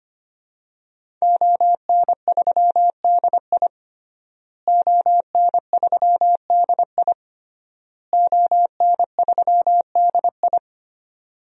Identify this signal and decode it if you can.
Morse code, heard as a keyed tone, about 25 wpm, 700 Hz. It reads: ON3DI ON3DI ON3DI
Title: DXCC-100 Call Signs - 5 Characters - Repeated 3x - 25wpm